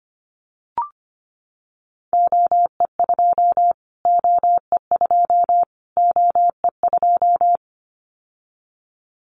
Morse code, heard as a keyed tone, about 25 wpm, 700 Hz. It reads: OE2 OE2 OE2
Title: Callsign Prefixes for Top 25 DXCC - Repeated 3x - 25wpm